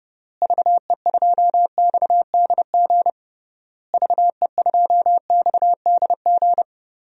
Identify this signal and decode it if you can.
Morse code, heard as a keyed tone, about 32 wpm, 700 Hz. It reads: VE2XDG VE2XDG
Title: DXCC-100 Call Signs - 6 Characters - Repeated 3x - 30wpm